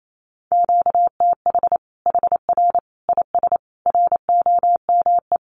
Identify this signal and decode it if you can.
Morse code, heard as a keyed tone, about 28 wpm, 700 Hz. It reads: QTH HR IS ROME
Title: Common QSO Phrases 28wpm